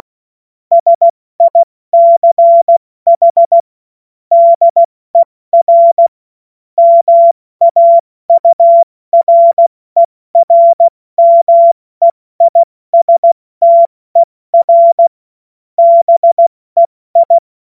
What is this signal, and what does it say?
Morse code, heard as a keyed tone, about 16 wpm, 680 Hz. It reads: SICH DER MAURERMEISTER BEI
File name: Morsekurs / Witze / Witz 17.Oktober 2023 70 F80